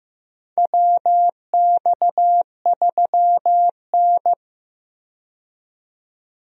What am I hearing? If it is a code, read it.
Morse code, heard as a keyed tone, about 15 wpm, 700 Hz. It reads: WX3N
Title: USA Call Signs - 4 Characters - 15wpm